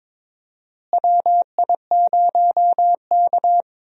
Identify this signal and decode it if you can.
Morse code, heard as a keyed tone, about 22 wpm, 700 Hz. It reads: WI0K
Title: USA Call Signs - 4 Characters - Repeated 2x - 22wpm